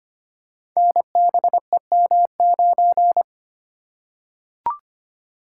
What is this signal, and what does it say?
Morse code, heard as a keyed tone, about 25 wpm, 700 Hz. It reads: NBEM9
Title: Code Groups - 5 Letter:Numbers - 25wpm